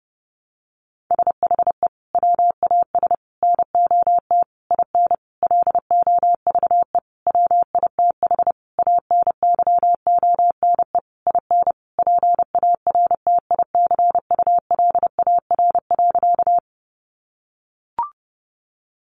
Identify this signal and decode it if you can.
Morse code, heard as a keyed tone, about 30 wpm, 700 Hz. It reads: SHE WAS NOT IN LOVE WITH ANYONE IN PARTICULAR.
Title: Sentences from Top 900 Words 30wpm